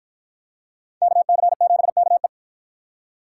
Morse code, heard as a keyed tone, about 53 wpm, 700 Hz. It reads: KC7KE